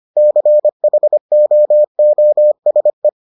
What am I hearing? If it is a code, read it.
Morse code, heard as a keyed tone, about 25 wpm, 600 Hz. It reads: CHOOSE